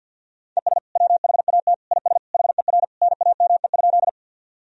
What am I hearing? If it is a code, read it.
Morse code, heard as a keyed tone, about 53 wpm, 700 Hz. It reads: ES WHAT IS HER NAME?
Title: Sentences from Top 200 Words and Abbreviations 50wpm